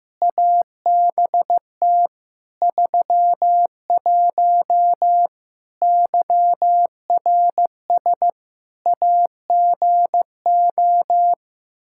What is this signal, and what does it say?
Morse code, heard as a keyed tone, about 16 wpm, 700 Hz. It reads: ABT 31 YRS AGO
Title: Practice QSO - Exchange 3 - More Info - 15wpm